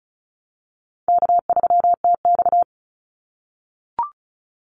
Morse code, heard as a keyed tone, about 34 wpm, 700 Hz. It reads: K3TX
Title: Call Signs - 35wpm